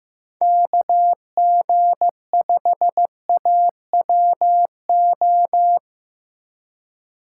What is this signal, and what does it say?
Morse code, heard as a keyed tone, about 15 wpm, 700 Hz. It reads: KG5AWO
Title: USA Call Signs - 6 Characters - 4x Character Spacing - 15wpm